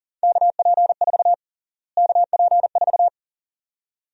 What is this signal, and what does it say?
Morse code, heard as a keyed tone, about 40 wpm, 700 Hz. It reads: KP4 KP4
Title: USA Call Sign Prefixes - Repeated 2x - 40wpm